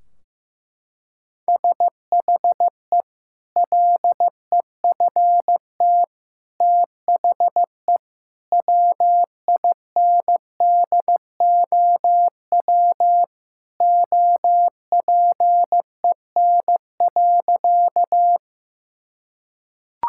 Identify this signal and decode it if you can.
Morse code, heard as a keyed tone, about 16 wpm, 700 Hz. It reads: SHE LEFT THE WINDOW OPEN.
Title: Sentences from Top 1600 Words 15wpm